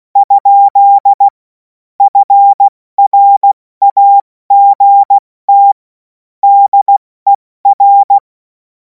Morse code, heard as a keyed tone, about 16 wpm, 800 Hz. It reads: ? FRAGT DER
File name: Morsekurs / Witze / Witz 28. April 2026 70 F80